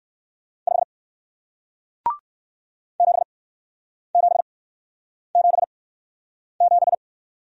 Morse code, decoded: 5 7777

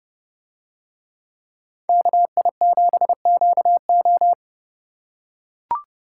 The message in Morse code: KI7QO

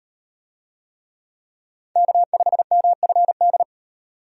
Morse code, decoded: K5MFD

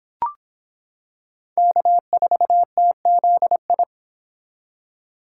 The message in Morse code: K4TZI

